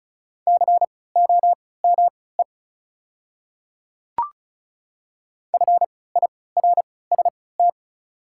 Morse code, heard as a keyed tone, about 36 wpm, spaced as for 20 wpm, 700 Hz. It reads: COME FIRST
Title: 3x Character Spacing - Top 200 Words - 35wpm